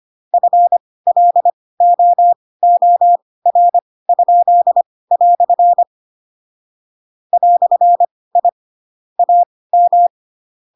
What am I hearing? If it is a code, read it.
Morse code, heard as a keyed tone, about 25 wpm, spaced as for 18 wpm, 700 Hz. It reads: FLOOR?" "I AM
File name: Pinocchio / Pinocchio ch2 12 25 ik1hge